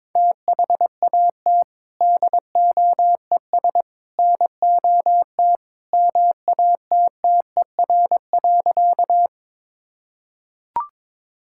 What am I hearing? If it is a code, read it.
Morse code, heard as a keyed tone, about 23 wpm, 700 Hz. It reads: THAT DOES NOT MATTER.
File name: practice / 22wpm / Sentences with 3-Word Phrases - 22wpm